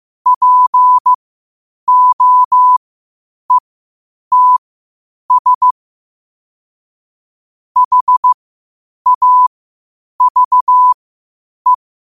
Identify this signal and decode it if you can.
Morse code, heard as a keyed tone, about 15 wpm, spaced as for 8 wpm, 1000 Hz. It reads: POETS HAVE
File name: qotdfiles / 8wpm